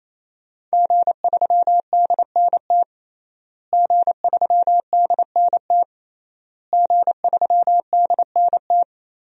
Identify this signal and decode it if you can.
Morse code, heard as a keyed tone, about 27 wpm, 700 Hz. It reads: G3DNT G3DNT G3DNT
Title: DXCC-100 Call Signs - 5 Characters - Repeated 3x - 28wpm